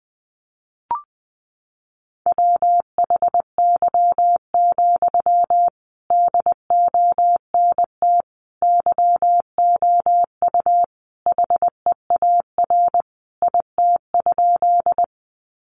Morse code, heard as a keyed tone, about 20 wpm, 700 Hz. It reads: WHY, DONT YOU HEAR IT?
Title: Sentences from Top 300 Words 20wpm